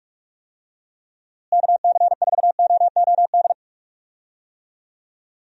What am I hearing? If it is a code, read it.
Morse code, heard as a keyed tone, about 46 wpm, 700 Hz. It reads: KC4OOD
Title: USA Call Signs - 6 Characters - Repeated 3x - 45wpm